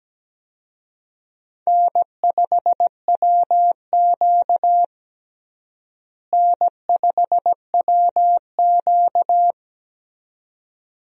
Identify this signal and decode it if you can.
Morse code, heard as a keyed tone, about 17 wpm, 700 Hz. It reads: N5WQ N5WQ
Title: USA Call Signs - 4 Characters - Repeated 2x - 17wpm